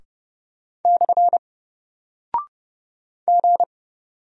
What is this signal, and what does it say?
Morse code, heard as a keyed tone, about 30 wpm, 700 Hz. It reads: / G
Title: Single Character (Letter:Number:Punctuation:Prosign) Alt 30wpm